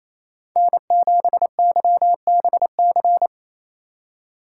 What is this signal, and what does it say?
Morse code, heard as a keyed tone, about 28 wpm, 700 Hz. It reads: N7YBC
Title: USA Call Signs - 5 Characters - 2x Character Spacing - 28wpm